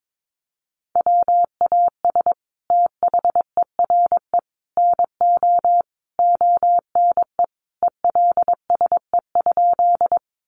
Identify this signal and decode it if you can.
Morse code, heard as a keyed tone, about 22 wpm, 700 Hz. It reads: WAS THERE NO ONE ELSE?
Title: Sentences from Top 800 Words 22wpm